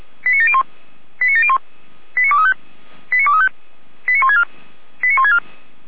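Five-tone selective call (ZVEI), heard as quick runs of five tones, tones of about 70 ms. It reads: 89701 89701 89235 89235 89164 89164
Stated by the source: ZVEI five-tone selective call 89590 received by SDR.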